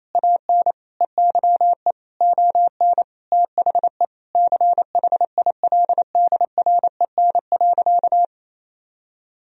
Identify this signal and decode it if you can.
Morse code, heard as a keyed tone, about 27 wpm, 700 Hz. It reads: AN EYE ON THE CHILDREN.
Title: Sentences from Top 300 Words and Abbreviations 28wpm